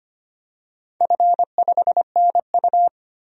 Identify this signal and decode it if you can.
Morse code, heard as a keyed tone, about 25 wpm, 700 Hz. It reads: F5NU
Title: DXCC-100 Call Signs - 4 Characters - Repeated 3x - 25wpm